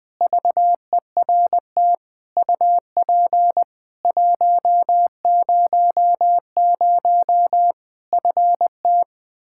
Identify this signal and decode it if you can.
Morse code, heard as a keyed tone, about 20 wpm, 700 Hz. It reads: VERT UP 100 FT
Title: Practice QSO - Exchange 2 - Info - 20wpm